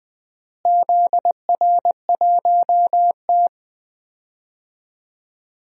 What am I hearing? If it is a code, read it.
Morse code, heard as a keyed tone, about 20 wpm, 700 Hz. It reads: ZR1T
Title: DXCC-100 Call Signs - 4 Characters - Repeated 3x - 20wpm